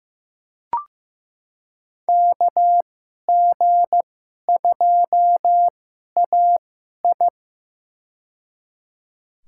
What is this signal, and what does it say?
Morse code, heard as a keyed tone, about 15 wpm, spaced as for 11 wpm, 700 Hz. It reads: KG2AI
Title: USA Call Signs - 5 Characters - 2x Character Spacing - 15wpm